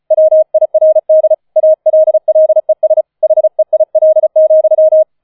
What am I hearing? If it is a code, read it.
Morse code, heard as a keyed tone, about 36 wpm, 630 Hz. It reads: WIRD ALLES HEIL,
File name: 004 Vierter Streich 35wpm